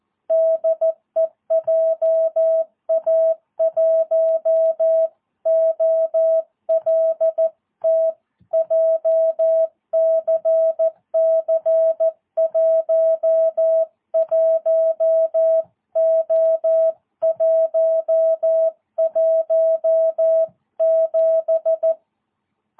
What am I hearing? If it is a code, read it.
Morse code, heard as a keyed tone, about 14 wpm, 660 Hz. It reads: DE JA1OLT JCC 11O117
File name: jh8cbh / 2022cw